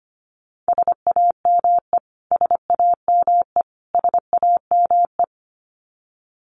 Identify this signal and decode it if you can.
Morse code, heard as a keyed tone, about 25 wpm, 700 Hz. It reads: SAME SAME SAME
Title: Top 300 Words Repeated 25wpm